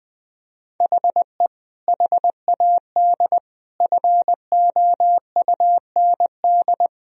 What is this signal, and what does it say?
Morse code, heard as a keyed tone, about 21 wpm, 700 Hz. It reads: HE HAD FOUND
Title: Sentences from Top 1000 Words - Review Part 3 - 20wpm